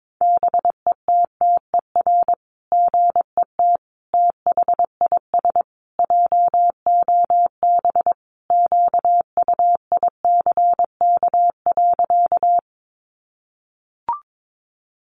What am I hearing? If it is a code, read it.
Morse code, heard as a keyed tone, about 22 wpm, 700 Hz. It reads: BETTER GET THIS JOB QUICK.
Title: Sentences from Top 600 Words 22wpm